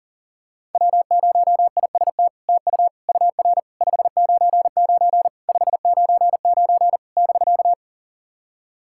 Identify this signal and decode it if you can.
Morse code, heard as a keyed tone, about 39 wpm, 700 Hz. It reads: W0IST TU UR 599 599 <BK>
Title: SOTA QSO Practice 40wpm